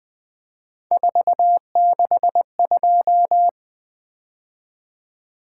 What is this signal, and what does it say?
Morse code, heard as a keyed tone, about 20 wpm, 700 Hz. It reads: 462
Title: Three Letter-Number Combinations - 20wpm